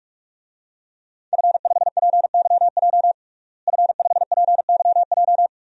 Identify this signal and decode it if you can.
Morse code, heard as a keyed tone, about 46 wpm, 700 Hz. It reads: F5PYJ F5PYJ
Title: DXCC-100 Call Signs - 5 Characters - Repeated 2x - 45wpm